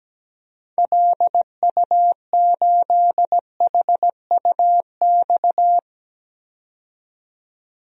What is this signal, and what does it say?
Morse code, heard as a keyed tone, about 17 wpm, 700 Hz. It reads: LU8HUX
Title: DXCC-100 Call Signs - 6 Characters - 17wpm